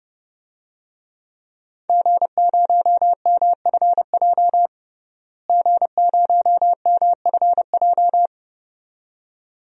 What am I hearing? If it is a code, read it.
Morse code, heard as a keyed tone, about 30 wpm, 700 Hz. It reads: G0MFJ G0MFJ
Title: DXCC-100 Call Signs - 5 Characters - Repeated 2x - 30wpm